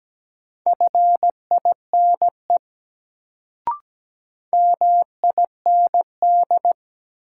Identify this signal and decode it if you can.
Morse code, heard as a keyed tone, about 17 wpm, 700 Hz. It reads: FINE MIND
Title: Sentences from Top 500 Words - New Word Drill - 17wpm